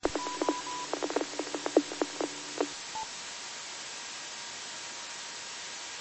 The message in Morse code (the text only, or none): none